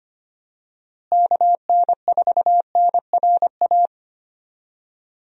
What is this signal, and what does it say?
Morse code, heard as a keyed tone, about 25 wpm, 700 Hz. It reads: KN4NRA
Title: USA Call Signs - 6 Characters - Repeated 3x - 25wpm